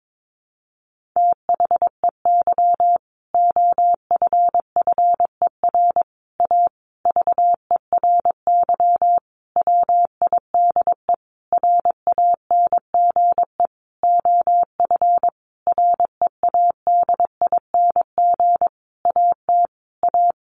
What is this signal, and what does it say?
Morse code, heard as a keyed tone, about 22 wpm, 700 Hz. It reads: THEY OFFER A VERY WIDE RANGE OF READING AT A
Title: Sentences from Top 900 Words 22wpm